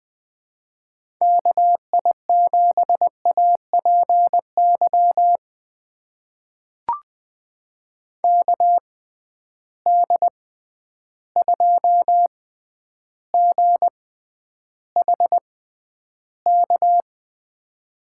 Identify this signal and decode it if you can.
Morse code, heard as a keyed tone, about 20 wpm, spaced as for 10 wpm, 700 Hz. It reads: KI7APY KD2GHK
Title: USA Call Signs - 6 Characters - 6x Character Spacing - 20wpm